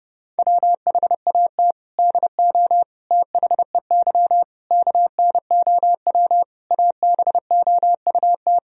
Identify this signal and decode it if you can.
Morse code, heard as a keyed tone, about 30 wpm, 700 Hz. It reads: WHAT DO THEY KNOW ABOUT
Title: Sentences from Top 1000 Words - Review Part 3 - 30wpm